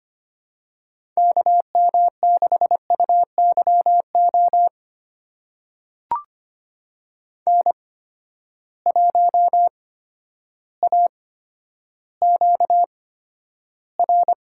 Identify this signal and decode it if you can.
Morse code, heard as a keyed tone, about 25 wpm, spaced as for 11 wpm, 700 Hz. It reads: KM6UYO N1AQR
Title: USA Call Signs - 8x Character Spacing - 25wpm